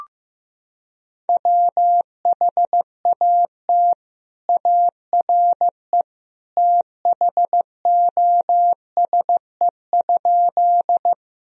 Morse code, heard as a keyed tone, about 15 wpm, 700 Hz. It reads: WHAT ARE THOSE?
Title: Sentences from Top 1000 Words - Review Part 1 - 15wpm